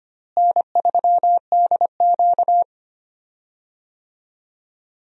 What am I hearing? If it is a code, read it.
Morse code, heard as a keyed tone, about 25 wpm, 700 Hz. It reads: N3DQ
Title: USA Call Signs - 4 Characters - Repeated 2x - 25wpm